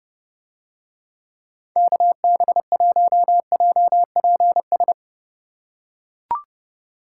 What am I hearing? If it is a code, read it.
Morse code, heard as a keyed tone, about 30 wpm, 700 Hz. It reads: KB1JPS